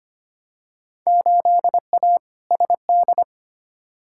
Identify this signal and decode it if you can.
Morse code, heard as a keyed tone, about 25 wpm, 700 Hz. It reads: 8A SD
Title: Field Day Exchanges Encore - 25wpm